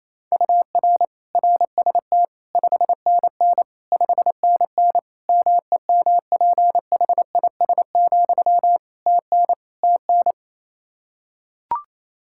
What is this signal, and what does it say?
Morse code, heard as a keyed tone, about 28 wpm, 700 Hz. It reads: UR RST 5NN 5NN MEMPHIS, TN TN
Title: Practice QSO - Exchange 1 - Beginning - 28wpm